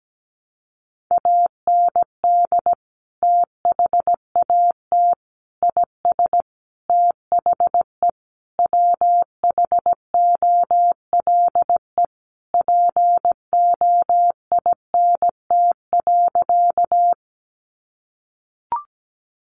Morse code, heard as a keyed tone, about 17 wpm, 700 Hz. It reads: AND THAT IS THE WHOLE POINT.
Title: Sentences from Top 400 Words 17wpm